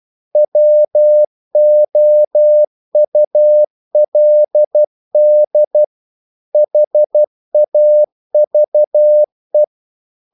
Morse code, heard as a keyed tone, about 12 wpm, 600 Hz. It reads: WOULD HAVE
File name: CW / Practice / EnglishQuotations0312